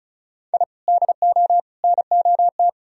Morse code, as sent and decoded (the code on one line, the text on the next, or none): .. / -.. --- / -. --- -
I DO NOT